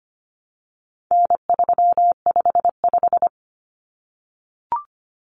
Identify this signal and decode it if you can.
Morse code, heard as a keyed tone, about 25 wpm, 700 Hz. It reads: N355
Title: Common Cut Numbers - 4-Digit 25wpm